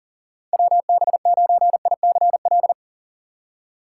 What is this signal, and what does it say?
Morse code, heard as a keyed tone, about 40 wpm, 700 Hz. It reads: WB9ICL
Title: USA Call Signs - 6 Characters - 3x Character Spacing - 40wpm